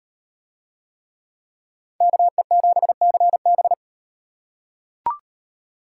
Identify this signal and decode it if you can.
Morse code, heard as a keyed tone, about 36 wpm, 700 Hz. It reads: KE7CB